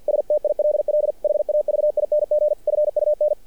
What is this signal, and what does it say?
Morse code, heard as a keyed tone, about 50 wpm, 600 Hz. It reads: STILL HAVING FUN
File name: archived / CRSnet